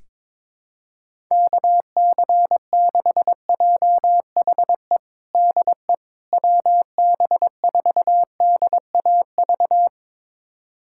Morse code, heard as a keyed tone, about 22 wpm, 700 Hz. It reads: KC6JHE DE WB4DAV
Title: USA Callsign DE Callsign 22wpm